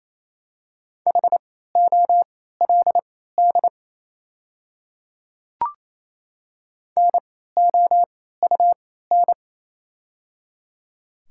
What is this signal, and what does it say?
Morse code, heard as a keyed tone, about 28 wpm, spaced as for 16 wpm, 700 Hz. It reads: HOLD NOUN
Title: 3x Character Spacing - Top 500 Words - 28wpm